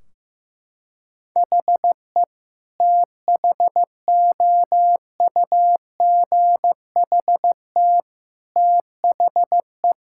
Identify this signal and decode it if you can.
Morse code, heard as a keyed tone, about 16 wpm, 700 Hz. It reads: HE THOUGHT THE
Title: Sentences from Top 1600 Words 15wpm